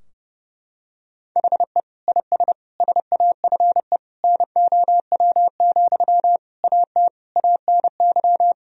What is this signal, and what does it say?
Morse code, heard as a keyed tone, about 32 wpm, 700 Hz. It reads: HE IS SAFE NOW, AT ANY
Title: Sentences from Top 1000 Words - Review Part 4 - 30wpm